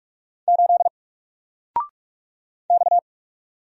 Morse code, heard as a keyed tone, about 46 wpm, 700 Hz. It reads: 8 X
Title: Single Letter-Number 45wpm